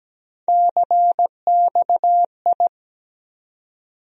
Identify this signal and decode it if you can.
Morse code, heard as a keyed tone, about 17 wpm, 700 Hz. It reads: CXI
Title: USA Call Sign Postfixes - 3 Letters 17wpm